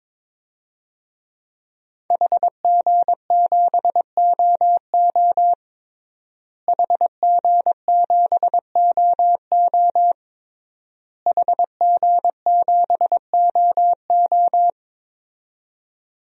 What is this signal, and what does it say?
Morse code, heard as a keyed tone, about 22 wpm, 700 Hz. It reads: HG7OO HG7OO HG7OO
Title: DXCC-100 Call Signs - 5 Characters - Repeated 3x - 22wpm